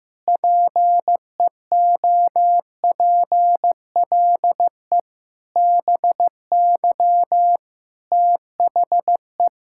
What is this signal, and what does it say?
Morse code, heard as a keyed tone, about 15 wpm, 700 Hz. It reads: PEOPLE BY THE
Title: Sentences from Top 1000 Words - Review Part 5 - 15wpm